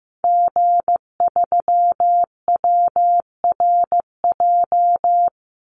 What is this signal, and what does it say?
Morse code, heard as a keyed tone, about 15 wpm, 700 Hz. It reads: G3WRJ
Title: Call Signs Repeated 3x - 15wpm